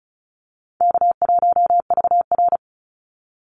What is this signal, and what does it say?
Morse code, heard as a keyed tone, about 34 wpm, 700 Hz. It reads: K1VR
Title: Easy Call Signs Repeated 2x - 35wpm